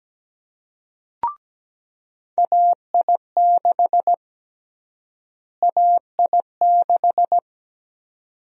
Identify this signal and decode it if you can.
Morse code, heard as a keyed tone, about 17 wpm, 700 Hz. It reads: AI6 AI6
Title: USA Call Sign Prefixes - Repeated 2x - 17wpm